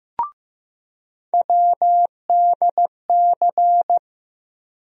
Morse code, heard as a keyed tone, about 15 wpm, 700 Hz. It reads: WDC
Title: USA Call Sign Postfixes - 3 Letters 15wpm